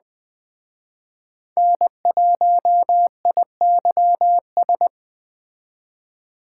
Morse code, heard as a keyed tone, about 21 wpm, 700 Hz. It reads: N1IYS
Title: Speed Racing - USA Call Signs 17wpm